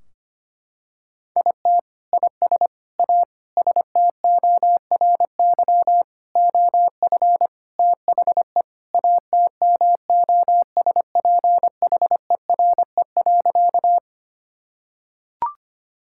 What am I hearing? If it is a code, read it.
Morse code, heard as a keyed tone, about 25 wpm, 700 Hz. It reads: IT IS A STORY OF THE ATMOSPHERE.
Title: Sentences from Top 1600 Words 25wpm